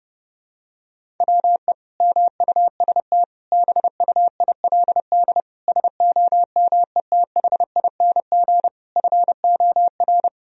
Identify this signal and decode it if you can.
Morse code, heard as a keyed tone, about 32 wpm, 700 Hz. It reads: WE MUST BUILD SOMETHING FOR
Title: Sentences from Top 1000 Words - Review Part 1 - 30wpm